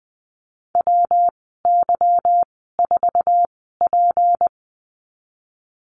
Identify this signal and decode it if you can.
Morse code, heard as a keyed tone, about 20 wpm, spaced as for 15 wpm, 700 Hz. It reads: WY4P
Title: 161 Lesson 26g - Letter g - Call Signs 20wpm